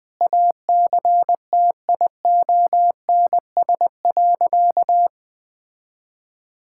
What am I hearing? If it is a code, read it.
Morse code, heard as a keyed tone, about 21 wpm, 700 Hz. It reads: ACTIONS.
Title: Sentences from Top 1000 Words - Review Part 6 - 20wpm